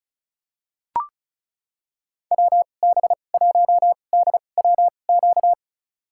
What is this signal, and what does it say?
Morse code, heard as a keyed tone, about 36 wpm, spaced as for 25 wpm, 700 Hz. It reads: WB1DWQ
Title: USA Call Signs - 6 Characters - 2x Character Spacing - 35wpm